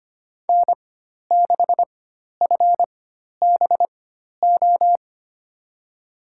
Morse code, heard as a keyed tone, about 25 wpm, spaced as for 12 wpm, 700 Hz. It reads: N6FBO
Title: USA Call Signs - 5 Characters - 4x Character Spacing - 25wpm